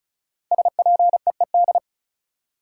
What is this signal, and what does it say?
Morse code, heard as a keyed tone, about 36 wpm, 700 Hz. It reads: SPEED
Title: Sentences from Top 800 Words - New Word Drill - 35wpm